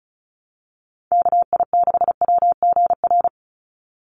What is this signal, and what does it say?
Morse code, heard as a keyed tone, about 34 wpm, 700 Hz. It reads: KI6WGR
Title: USA Call Signs 35wpm